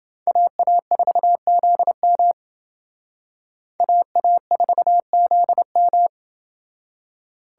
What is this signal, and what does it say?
Morse code, heard as a keyed tone, about 28 wpm, 700 Hz. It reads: AA4ZM AA4ZM